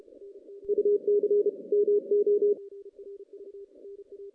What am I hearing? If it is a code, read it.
Morse code, heard as a keyed tone, about 33 wpm, 410 Hz. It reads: UC MO
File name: 2016 JAN CW NAQP / 2016 Jan naqp k5ot - 0-10